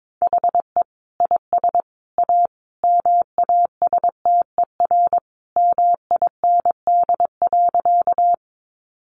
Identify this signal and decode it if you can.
Morse code, heard as a keyed tone, about 22 wpm, 700 Hz. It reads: HE IS A MASTER MIND.
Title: Sentences from Top 1300 Words 22wpm